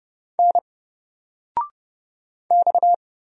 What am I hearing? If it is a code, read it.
Morse code, heard as a keyed tone, about 30 wpm, 700 Hz. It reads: N X
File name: Single-Letter-Number-30wpm